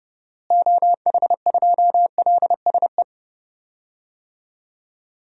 Morse code, heard as a keyed tone, about 30 wpm, 700 Hz. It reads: OH2LSE
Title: DXCC-100 Call Signs - 6 Characters - Repeated 3x - 30wpm